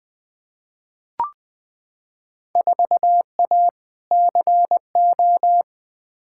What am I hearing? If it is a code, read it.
Morse code, heard as a keyed tone, about 21 wpm, 700 Hz. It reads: 4A CO
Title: Field Day Exchanges - 20wpm